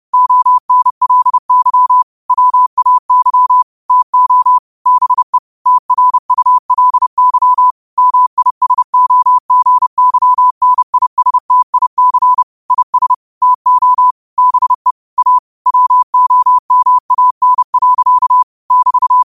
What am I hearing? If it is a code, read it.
Morse code, heard as a keyed tone, about 30 wpm, 1000 Hz. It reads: ONLY WAY TO BE TRULY MISOGYNISTIC IS TO BE A WOMAN. <BT>
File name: qotdfiles / 30wpm